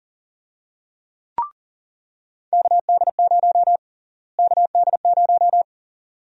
Morse code, heard as a keyed tone, about 40 wpm, 700 Hz. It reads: KD0 KD0
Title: USA Call Sign Prefixes - Repeated 2x - 40wpm